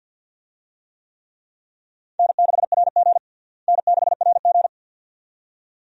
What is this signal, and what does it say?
Morse code, heard as a keyed tone, about 48 wpm, 690 Hz. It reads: N6RG N6RG
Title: USA Call Signs - 4 Characters - Repeated 2x - 50wpm